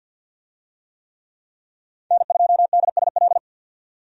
Morse code, heard as a keyed tone, about 50 wpm, 700 Hz. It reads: N2DSL
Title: USA Call Signs - 5 Characters - 50wpm